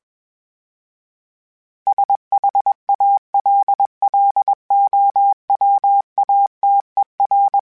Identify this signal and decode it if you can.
Morse code, heard as a keyed tone, about 21 wpm, 800 Hz. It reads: SHALLOWATER
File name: fromSMs / FromSMs mp3s / Cities@20